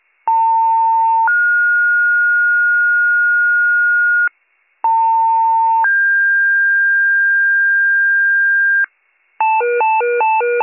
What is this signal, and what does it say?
Two-tone page, heard as a one-second tone, then a three-second tone, about 900 then 1430 Hz, about 900 then 1600 Hz, with noise. It is heard from 0.3 to 8.9 s.